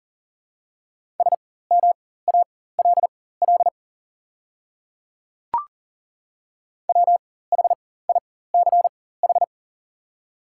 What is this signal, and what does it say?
Morse code, heard as a keyed tone, about 40 wpm, spaced as for 19 wpm, 690 Hz. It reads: SMALL WHICH
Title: 4x Character Spacing - Top 300 Words - 40wpm